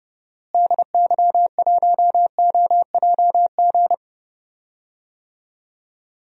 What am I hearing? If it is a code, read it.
Morse code, heard as a keyed tone, about 30 wpm, 700 Hz. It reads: DY1OJG
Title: DXCC-100 Call Signs - 6 Characters - Repeated 2x - 30wpm